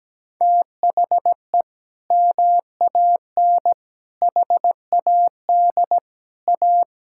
Morse code, heard as a keyed tone, about 17 wpm, 700 Hz. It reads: THE MAN HAD A
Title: Sentences from Top 1000 Words - Review Part 6 - 17wpm